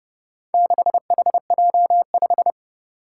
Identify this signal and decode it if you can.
Morse code, heard as a keyed tone, about 31 wpm, 700 Hz. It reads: 6HJ5
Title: Four Challenging Letter-Number Combinations 30wpm